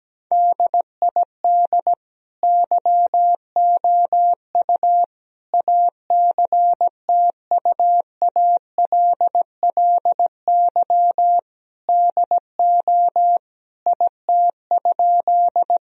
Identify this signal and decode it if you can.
Morse code, heard as a keyed tone, about 17 wpm, 700 Hz. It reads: DID YOU ACTUALLY DO IT?
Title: Sentences from Top 1000 Words - Review Part 4 - 17wpm